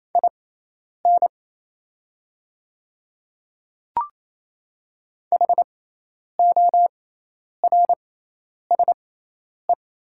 Morse code, heard as a keyed tone, about 28 wpm, spaced as for 10 wpm, 700 Hz. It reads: IN HORSE
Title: 6x Character Spacing - Top 300 Words - 28wpm